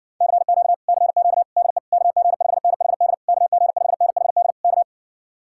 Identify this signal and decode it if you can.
Morse code, heard as a keyed tone, about 59 wpm, 700 Hz. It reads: CQ CQ DE KG5NHD KG5NHD K